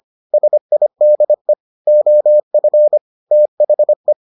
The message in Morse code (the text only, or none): SIDE OF THE